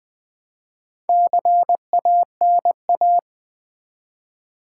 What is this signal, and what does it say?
Morse code, heard as a keyed tone, about 20 wpm, 700 Hz. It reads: CANA
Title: Build A Word - 1 Letter At A Time - 5-Letter Words - 20wpm